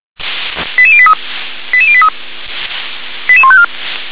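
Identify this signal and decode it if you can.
Five-tone selective call (ZVEI), heard as quick runs of five tones, tones of about 70 ms. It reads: 88083 88083 88154
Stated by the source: ZVEI five-tone selective call received on SDR.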